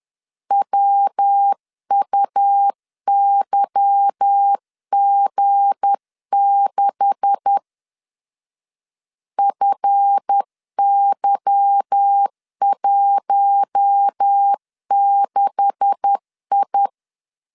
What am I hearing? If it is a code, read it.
Morse code, heard as a keyed tone, about 10 wpm, 800 Hz. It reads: WUYG6 FY16I
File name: LEZIONI / 9 286 10wpm 2735